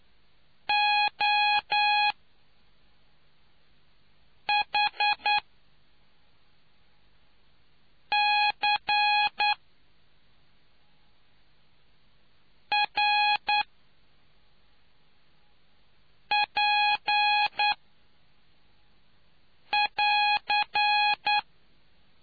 Morse code, heard as a keyed tone, about 9 wpm, spaced as for 3 wpm, 830 Hz. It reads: OHCRP<AR>